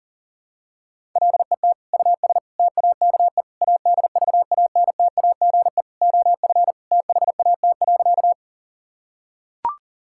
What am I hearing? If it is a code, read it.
Morse code, heard as a keyed tone, about 40 wpm, 700 Hz. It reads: LET US TAKE ADVANTAGE OF THAT.
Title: Sentences from Top 1000 Words - Review Part 6 - 40wpm